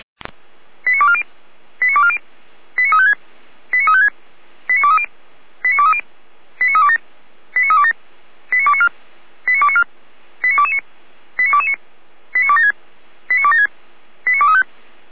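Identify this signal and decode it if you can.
Five-tone selective call (ZVEI), heard as quick runs of five tones, tones of about 70 ms. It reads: 89240 89240 89356 89356 89230 89230 89327 89327 89294 89294 89229 89229 89276 89276 89235
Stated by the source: ZVEI five-tone selective call 89270, received by SDR.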